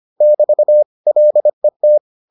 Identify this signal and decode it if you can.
Morse code, heard as a keyed tone, about 25 wpm, 600 Hz. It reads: <BT> LET